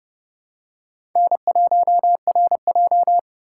Morse code, heard as a keyed tone, about 30 wpm, 700 Hz. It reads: N1RJ
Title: USA Call Signs - 4 Characters - 4x Character Spacing - 30wpm